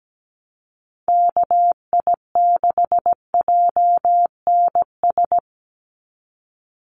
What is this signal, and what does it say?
Morse code, heard as a keyed tone, about 17 wpm, 700 Hz. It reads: KI6JNS
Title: USA Call Signs 17wpm